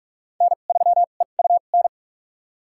Morse code, heard as a keyed tone, about 46 wpm, spaced as for 33 wpm, 700 Hz. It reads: N3EUN
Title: USA Call Signs - 2x Character Spacing - 45wpm